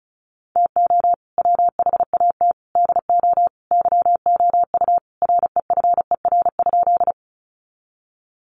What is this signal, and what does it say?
Morse code, heard as a keyed tone, about 34 wpm, 700 Hz. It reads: TO WHAT DO YOU REFER?
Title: Sentences from Top 1000 Words 35wpm